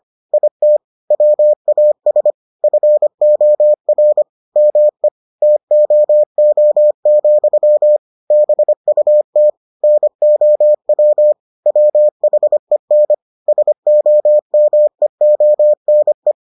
Morse code, heard as a keyed tone, about 25 wpm, 600 Hz. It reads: IT WAS FOR ME TOO, BUT NOW WHEN SOMEONE